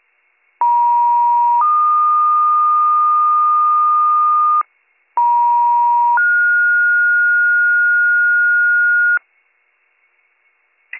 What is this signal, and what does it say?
Two-tone page, heard as a one-second tone, then a three-second tone, about 950 then 1220 Hz, about 950 then 1430 Hz, with noise. It is heard from 0.6 to 9.2 s.